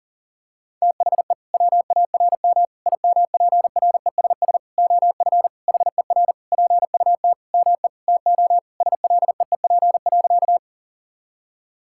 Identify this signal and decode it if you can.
Morse code, heard as a keyed tone, about 40 wpm, 700 Hz. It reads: THE WARM IMPRESS OF HER PUT ME TO SLEEP.
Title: Sentences from Top 1600 Words 40wpm